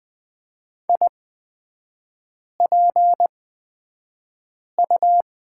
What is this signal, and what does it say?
Morse code, heard as a keyed tone, about 20 wpm, spaced as for 5 wpm, 700 Hz. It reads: IPU
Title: 236 Lesson 38e - Letter q - Three Characters 20wpm